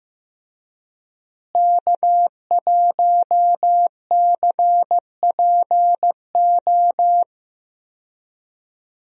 Morse code, heard as a keyed tone, about 16 wpm, 700 Hz. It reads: K1CPO
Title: USA Call Signs - 5 Characters - Repeated 2x - 15wpm